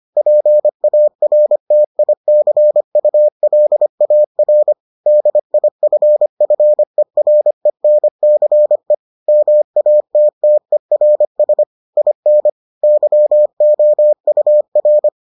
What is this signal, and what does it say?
Morse code, heard as a keyed tone, about 25 wpm, 600 Hz. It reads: PARTICULAR DIFFERENCE MATTERS IN YOUR